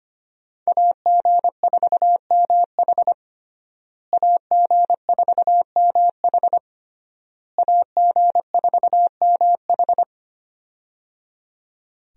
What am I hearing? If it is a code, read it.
Morse code, heard as a keyed tone, about 25 wpm, 700 Hz. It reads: AG4MH AG4MH AG4MH
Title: USA Call Signs - 5 Characters - Repeated 3x - 25wpm